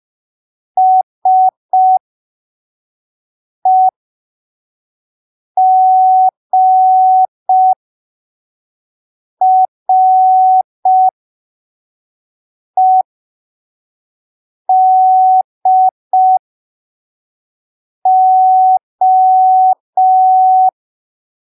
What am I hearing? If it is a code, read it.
Morse code, heard as a keyed tone, about 5 wpm, spaced as for 3 wpm, 750 Hz. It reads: SEGREDO
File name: arquivos / texto / texto 20 05ppm